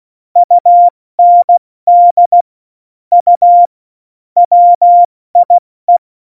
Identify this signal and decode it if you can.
Morse code, heard as a keyed tone, about 16 wpm, spaced as for 14 wpm, 700 Hz. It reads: UND U WIE
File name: Morsekurs / Witze / Witz 05. Juni 2018, 70 F80